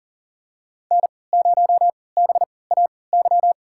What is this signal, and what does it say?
Morse code, heard as a keyed tone, about 40 wpm, spaced as for 23 wpm, 700 Hz. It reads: N0BAY
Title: USA Call Signs - 5 Characters - 3x Character Spacing - 40wpm